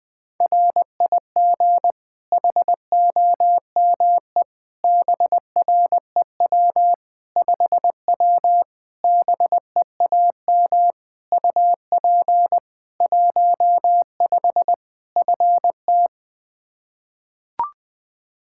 Morse code, heard as a keyed tone, about 20 wpm, 700 Hz. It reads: RIG HOME BREW 5W BEAM UP 15 FT
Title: K1USN SST QSO Practice 20wpm